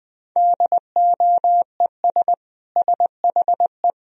Morse code, heard as a keyed tone, about 20 wpm, 700 Hz. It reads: DOES SHE
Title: Sentences from Top 1000 Words - Review Part 2 - 20wpm